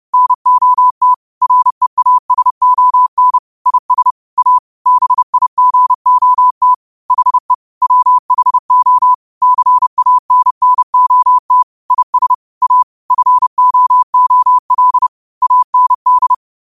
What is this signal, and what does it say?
Morse code, heard as a keyed tone, about 30 wpm, 1000 Hz. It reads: NOT REASON IS A BIGOT HE WHO CANNOT IS A FOOL AND